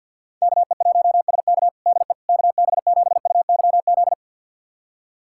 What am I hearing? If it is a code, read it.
Morse code, heard as a keyed tone, about 51 wpm, 700 Hz. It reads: KE1SK DE KB7UYZ
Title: USA Callsign DE Callsign 50wpm